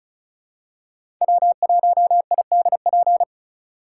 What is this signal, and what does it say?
Morse code, heard as a keyed tone, about 36 wpm, 700 Hz. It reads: W1IDP
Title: USA Call Signs - 5 Characters - 3x Character Spacing - 35wpm